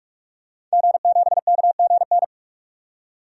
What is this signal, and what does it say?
Morse code, heard as a keyed tone, about 44 wpm, 690 Hz. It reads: G7KGN
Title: DXCC-100 Call Signs - 5 Characters - 45wpm